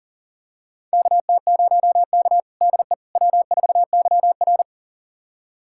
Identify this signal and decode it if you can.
Morse code, heard as a keyed tone, about 37 wpm, 690 Hz. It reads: KT0K DE W4YR
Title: USA Callsign DE Callsign - 4 Characters - 40wpm